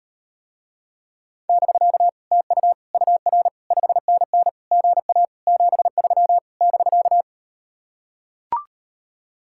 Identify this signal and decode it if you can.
Morse code, heard as a keyed tone, about 38 wpm, 700 Hz. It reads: <BK> TU UR 5NN GA 73 <BK>